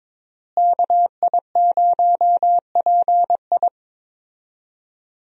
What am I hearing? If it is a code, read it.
Morse code, heard as a keyed tone, about 23 wpm, 700 Hz. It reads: KI0PI
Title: USA Call Signs - 5 Characters - Repeated 2x - 22wpm